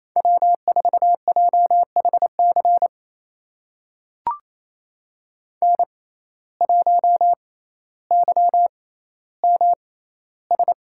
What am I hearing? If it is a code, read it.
Morse code, heard as a keyed tone, about 28 wpm, spaced as for 14 wpm, 700 Hz. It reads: W4JHC N1YMS
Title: USA Call Signs - 5 Characters - 6x Character Spacing - 28wpm